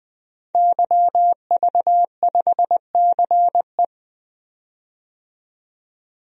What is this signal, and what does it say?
Morse code, heard as a keyed tone, about 20 wpm, 700 Hz. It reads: YV5CE
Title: DXCC-100 Call Signs - 5 Characters - 20wpm